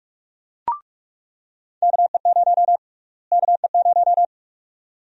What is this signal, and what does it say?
Morse code, heard as a keyed tone, about 44 wpm, 700 Hz. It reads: KE0 KE0
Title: USA Call Sign Prefixes - Repeated 2x - 45wpm